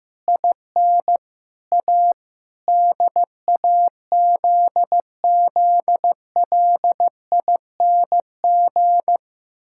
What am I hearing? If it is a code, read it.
Morse code, heard as a keyed tone, about 15 wpm, 700 Hz. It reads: IN A DAZZLING
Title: Pangram Challenge v2 - 15wpm